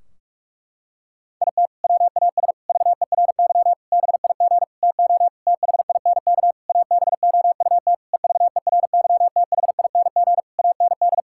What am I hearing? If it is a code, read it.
Morse code, heard as a keyed tone, about 41 wpm, 700 Hz. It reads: IT WAS VERY BIG TO THINK ABOUT EVERYTHING AND
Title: Sentences from Top 1600 Words 45wpm